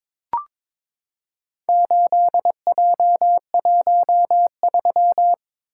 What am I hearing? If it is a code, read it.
Morse code, heard as a keyed tone, about 22 wpm, 700 Hz. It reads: 8J13
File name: practice / 22wpm / Four Letter-Number Combinations - 22wpm